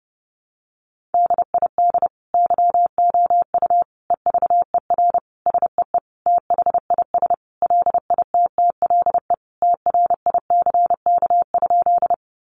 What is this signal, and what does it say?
Morse code, heard as a keyed tone, about 28 wpm, 700 Hz. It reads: DID YOU EVER SEE THIS LITTLE TRICK?
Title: Sentences from Top 1500 Words 30wpm